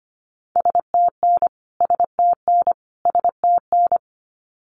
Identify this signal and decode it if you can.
Morse code, heard as a keyed tone, about 25 wpm, 700 Hz. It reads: STN STN STN
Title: Abbreviations Repeated 3x - 25wpm